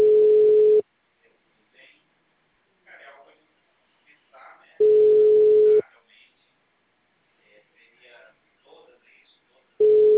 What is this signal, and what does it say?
Ringback tone (Europe, 425 Hz), with speech.